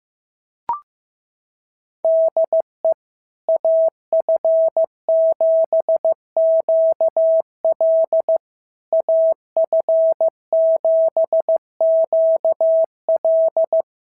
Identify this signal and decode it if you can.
Morse code, heard as a keyed tone, about 16 wpm, 650 Hz. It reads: DE AF7QL AF7QL
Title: SOTA QSO Practice 15wpm